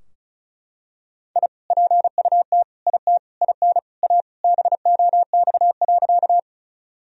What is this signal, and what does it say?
Morse code, heard as a keyed tone, about 36 wpm, 700 Hz. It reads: I PUT IT IN A BOX.
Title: Sentences from Top 1000 Words - Review Part 2 - 35wpm